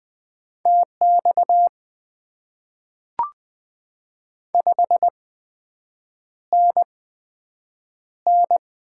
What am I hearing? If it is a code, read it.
Morse code, heard as a keyed tone, about 20 wpm, spaced as for 7 wpm, 700 Hz. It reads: TX 5NN
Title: Top 90 QSO Elements - 8x Character Spacing - 20wpm